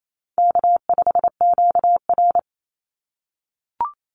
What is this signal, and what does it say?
Morse code, heard as a keyed tone, about 28 wpm, 700 Hz. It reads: K5QR
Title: NA SOTA Chaser Callsigns 28wpm